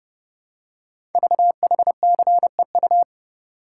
Morse code, heard as a keyed tone, about 30 wpm, 700 Hz. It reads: VHCEU
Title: Code Groups - 5 Letters - 30wpm